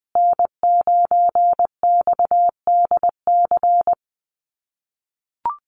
Morse code, heard as a keyed tone, about 20 wpm, 700 Hz. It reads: N9XDC
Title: USA Call Signs Repeated 2x 20wpm